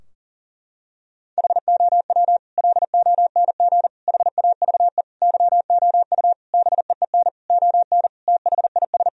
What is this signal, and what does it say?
Morse code, heard as a keyed tone, about 40 wpm, 700 Hz. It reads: HOW LONG HAVE YOU BEEN ON THIS
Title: Sentences from Top 1000 Words - Review Part 3 - 40wpm